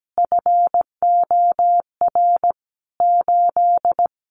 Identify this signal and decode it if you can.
Morse code, heard as a keyed tone, about 17 wpm, 700 Hz. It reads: FOR 8
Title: Additional Easy Phrases 17wpm